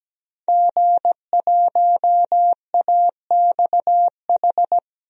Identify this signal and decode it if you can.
Morse code, heard as a keyed tone, about 17 wpm, 700 Hz. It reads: G1AXH
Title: DXCC-100 Call Signs - 5 Characters - Repeated 2x - 17wpm